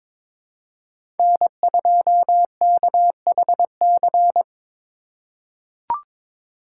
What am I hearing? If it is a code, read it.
Morse code, heard as a keyed tone, about 22 wpm, 700 Hz. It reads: N2KHC
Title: USA Call Signs - 5 Characters - 4x Character Spacing - 22wpm